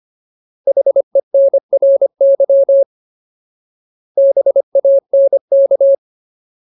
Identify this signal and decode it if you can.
Morse code, heard as a keyed tone, about 25 wpm, 550 Hz. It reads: HENRY BANK
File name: Ultimate Word List / 25WPM Ultimate